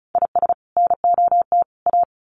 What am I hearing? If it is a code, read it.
Morse code, heard as a keyed tone, about 34 wpm, 700 Hz. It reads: IS NOT A
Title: Sets of 3 Words 35wpm